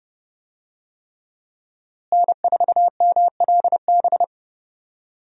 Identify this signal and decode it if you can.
Morse code, heard as a keyed tone, about 31 wpm, 700 Hz. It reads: N4MLB
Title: USA Call Signs - 5 Characters - Repeated 3x - 30wpm